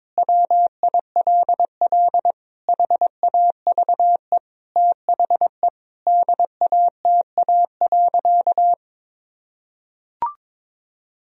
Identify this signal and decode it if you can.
Morse code, heard as a keyed tone, about 22 wpm, 700 Hz. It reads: WILL HAVE THE DATA.
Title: Sentences from Top 1000 Words - Review Part 3 - 22wpm